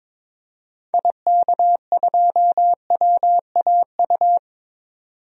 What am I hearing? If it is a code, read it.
Morse code, heard as a keyed tone, about 22 wpm, 700 Hz. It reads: IK2WAU
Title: DXCC-100 Call Signs - 6 Characters - 22wpm